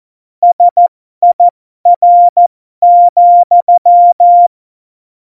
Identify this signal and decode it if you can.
Morse code, heard as a keyed tone, about 14 wpm, spaced as for 12 wpm, 700 Hz. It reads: SIR,
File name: Morsekurs / Witze